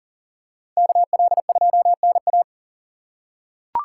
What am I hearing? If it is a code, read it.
Morse code, heard as a keyed tone, about 40 wpm, 700 Hz. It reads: KL2NA
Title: USA Call Signs - 5 Characters - 6x Character Spacing - 40wpm